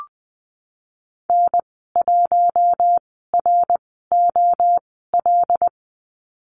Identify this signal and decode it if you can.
Morse code, heard as a keyed tone, about 20 wpm, spaced as for 15 wpm, 700 Hz. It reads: N1ROL
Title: 088 Lesson 15g - Number 5 - Call Signs 20wpm